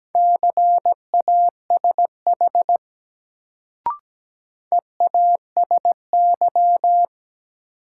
Morse code, heard as a keyed tone, about 17 wpm, 700 Hz. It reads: CASH EASY
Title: Sentences from Top 700 Words - New Word Drill - 17wpm